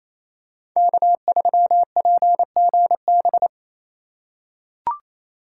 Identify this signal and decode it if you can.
Morse code, heard as a keyed tone, about 28 wpm, 700 Hz. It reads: K3PGB
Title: USA Call Signs - 5 Characters - 4x Character Spacing - 28wpm